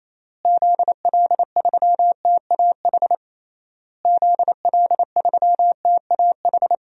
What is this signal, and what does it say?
Morse code, heard as a keyed tone, about 28 wpm, 700 Hz. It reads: ZL3TAH ZL3TAH
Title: DXCC-100 Call Signs - 6 Characters - Repeated 3x - 28wpm